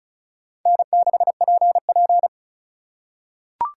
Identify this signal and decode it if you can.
Morse code, heard as a keyed tone, about 36 wpm, 700 Hz. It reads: N6PP
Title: USA Call Signs - 4 Characters - 3x Character Spacing - 35wpm